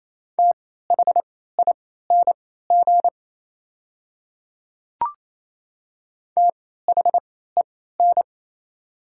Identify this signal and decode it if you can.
Morse code, heard as a keyed tone, about 28 wpm, spaced as for 16 wpm, 700 Hz. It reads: THING THEN
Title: 3x Character Spacing - Top 300 Words - 28wpm